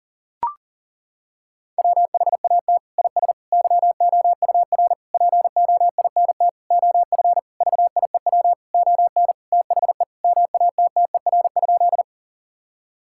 Morse code, heard as a keyed tone, about 37 wpm, 700 Hz. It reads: WHAT IS YOUR POINT OF VIEW ON THE MATTER?
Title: Sentences from Top 1000 Words - Review Part 6 - 40wpm